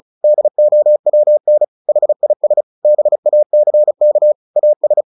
Morse code, heard as a keyed tone, about 35 wpm, 600 Hz. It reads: DOWN HIS BACK AS
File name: the old man and the sea / the old man and the sea 1 35wpm